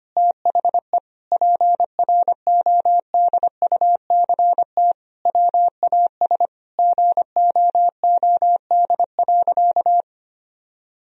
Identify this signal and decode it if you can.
Morse code, heard as a keyed tone, about 25 wpm, 700 Hz. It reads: THE PRODUCT WAS GOOD.
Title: Sentences from Top 1000 Words - Review Part 3 - 25wpm